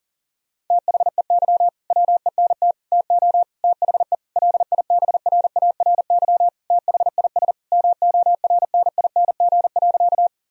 Morse code, heard as a keyed tone, about 40 wpm, 700 Hz. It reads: THEY WENT TO THE LIBRARY THIS MORNING.
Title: Sentences from Top 1000 Words - Review Part 4 - 40wpm